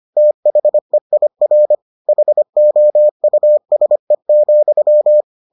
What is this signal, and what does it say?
Morse code, heard as a keyed tone, about 25 wpm, 600 Hz. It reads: THEIR HOUSE,